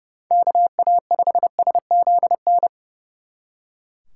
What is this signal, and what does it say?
Morse code, heard as a keyed tone, about 37 wpm, 700 Hz. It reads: KA5SZN
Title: Speed Racing - USA 2x3 Call Signs 30wpm